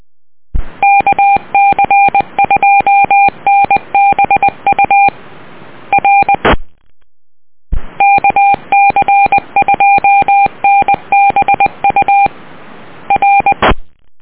Morse code, heard as a keyed tone, about 20 wpm, 800 Hz. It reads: KC2NBU R KC2NBU R